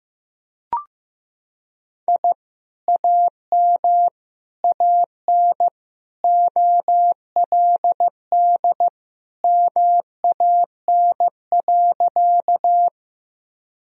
Morse code, heard as a keyed tone, about 15 wpm, 700 Hz. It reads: I AM AN OLD MAN.
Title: Sentences from Top 1000 Words - Review Part 2 - 15wpm